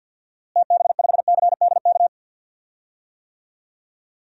Morse code, heard as a keyed tone, about 51 wpm, 700 Hz. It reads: TB5CDK
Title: DXCC-100 Call Signs - 6 Characters - Repeated 2x - 50wpm